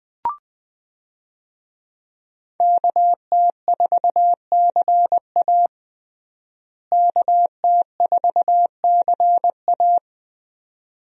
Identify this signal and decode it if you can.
Morse code, heard as a keyed tone, about 20 wpm, 700 Hz. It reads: KT4CA KT4CA
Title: USA Call Signs - 5 Characters - Repeated 2x - 20wpm